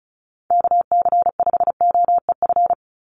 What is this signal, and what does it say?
Morse code, heard as a keyed tone, about 34 wpm, 700 Hz. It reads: KC5OEF
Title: USA Call Signs 35wpm